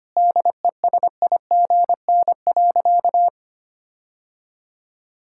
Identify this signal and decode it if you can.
Morse code, heard as a keyed tone, about 25 wpm, 700 Hz. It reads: DESIGN.
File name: practice / 25wpm / All-Age Appropriate Jokes - 25wpm